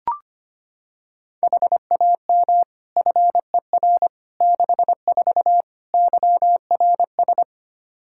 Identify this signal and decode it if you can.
Morse code, heard as a keyed tone, about 25 wpm, 700 Hz. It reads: HAM FER 64 YRS
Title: Practice QSO - Exchange 3 - More Info - 25wpm